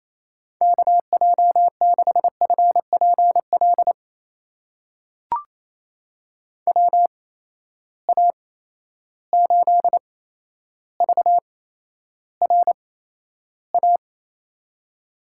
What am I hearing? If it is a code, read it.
Morse code, heard as a keyed tone, about 28 wpm, spaced as for 12 wpm, 700 Hz. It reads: KJ6FPL WA8VRA
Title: USA Call Signs - 6 Characters - 8x Character Spacing - 28wpm